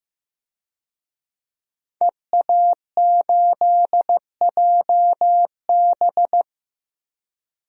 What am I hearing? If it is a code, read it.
Morse code, heard as a keyed tone, about 15 wpm, 700 Hz. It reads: EA8JB
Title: DXCC-100 Call Signs - 5 Characters - 15wpm